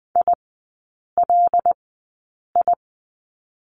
Morse code, heard as a keyed tone, about 20 wpm, spaced as for 8 wpm, 700 Hz. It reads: ILI
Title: 145 Lesson 24e - Letter y - Three Characters 20wpm